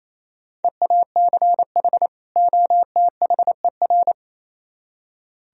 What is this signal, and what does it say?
Morse code, heard as a keyed tone, about 37 wpm, 700 Hz. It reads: EACH OTHER